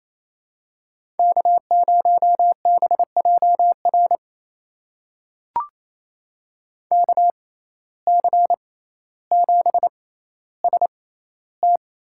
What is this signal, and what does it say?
Morse code, heard as a keyed tone, about 28 wpm, spaced as for 14 wpm, 700 Hz. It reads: K0BJR KC7ST
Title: USA Call Signs - 6x Character Spacing - 28wpm